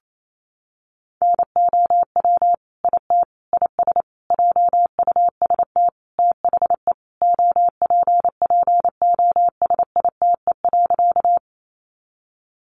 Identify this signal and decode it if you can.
Morse code, heard as a keyed tone, about 28 wpm, 700 Hz. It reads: NOW IT IS JUST THE OPPOSITE.
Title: Sentences from Top 1300 Words 28wpm